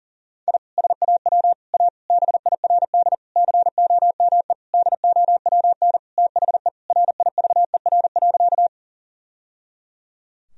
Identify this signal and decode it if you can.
Morse code, heard as a keyed tone, about 37 wpm, 700 Hz. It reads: I SAW A BIRD COME DOWN THE RIVER.
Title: Sentences from Top 1000 Words - Review Part 2 - 40wpm